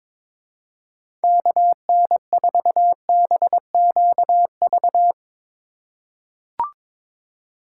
Morse code, heard as a keyed tone, about 23 wpm, 700 Hz. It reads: KN4BQV
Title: USA Call Signs - 6 Characters - 2x Character Spacing - 22wpm